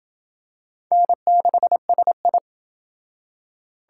Morse code, heard as a keyed tone, about 27 wpm, 700 Hz. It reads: N6SI